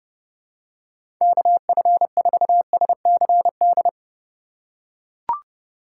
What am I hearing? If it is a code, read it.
Morse code, heard as a keyed tone, about 30 wpm, 700 Hz. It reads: KF4SCD
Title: USA Call Signs - 4x Character Spacing - 30wpm